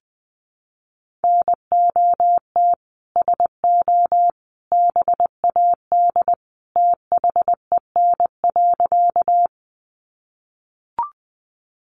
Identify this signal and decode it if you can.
Morse code, heard as a keyed tone, about 20 wpm, 700 Hz. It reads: NOT SO BAD THEN.
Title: Sentences from Top 700 Words 20wpm